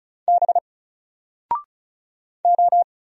Morse code, heard as a keyed tone, about 36 wpm, 700 Hz. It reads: B O
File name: Single-Letter-Number-35wpm